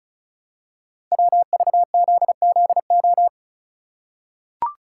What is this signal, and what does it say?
Morse code, heard as a keyed tone, about 36 wpm, 690 Hz. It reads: WVZZO
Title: Code Groups - 5 Letters - 35wpm